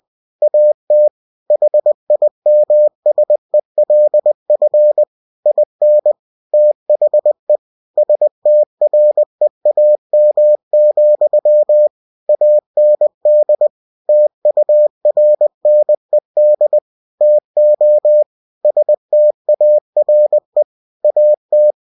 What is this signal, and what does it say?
Morse code, heard as a keyed tone, about 20 wpm, 600 Hz. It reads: AT HIMSELF IN THE STREAM, AND TURNED TO STARE AT
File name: cw-books / winnie-the-pooh / winnie-20wpm-page-0054